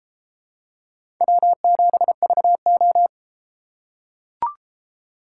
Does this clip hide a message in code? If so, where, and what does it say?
Yes: Morse code from 1.2 to 3.1 s, heard as a keyed tone, about 31 wpm, 700 Hz. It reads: W7VO